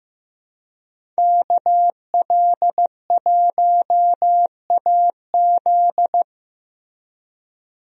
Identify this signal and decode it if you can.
Morse code, heard as a keyed tone, about 15 wpm, 700 Hz. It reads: KL1AZ
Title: USA Call Signs - 5 Characters - Repeated 3x - 15wpm